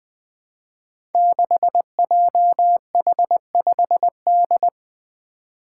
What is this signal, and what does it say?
Morse code, heard as a keyed tone, about 20 wpm, 700 Hz. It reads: 6JH5D
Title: Five Challenging Letter-Number Combinations 20wpm